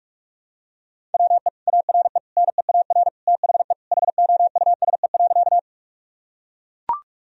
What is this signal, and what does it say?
Morse code, heard as a keyed tone, about 42 wpm, 700 Hz. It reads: WE ARE NEAR THE HOUSE.
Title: Sentences from Top 1000 Words - Review Part 1 - 45wpm